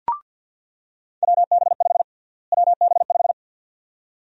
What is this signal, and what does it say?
Morse code, heard as a keyed tone, about 47 wpm, 700 Hz. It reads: WB5 WB5
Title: USA Call Sign Prefixes - Repeated 2x - 50wpm